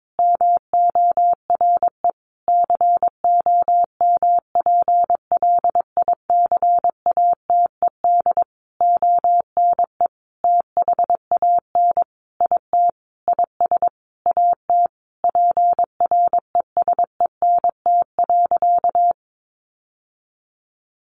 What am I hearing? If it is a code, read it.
Morse code, heard as a keyed tone, about 22 wpm, 700 Hz. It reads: MORE COMPLICATED ONE THAN IT IS AT PRESENT.
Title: Sentences from Top 1200 Words 22wpm